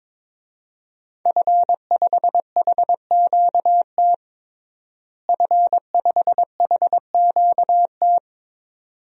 Morse code, heard as a keyed tone, about 22 wpm, 700 Hz. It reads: F5HQT F5HQT
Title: DXCC-100 Call Signs - 5 Characters - Repeated 2x - 22wpm